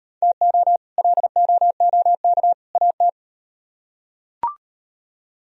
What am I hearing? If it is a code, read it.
Morse code, heard as a keyed tone, about 38 wpm, 700 Hz. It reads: TO LOOK AT